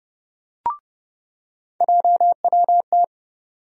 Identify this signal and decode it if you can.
Morse code, heard as a keyed tone, about 30 wpm, 700 Hz. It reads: JWT
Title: USA Call Sign Postfixes - 3 Letters 30wpm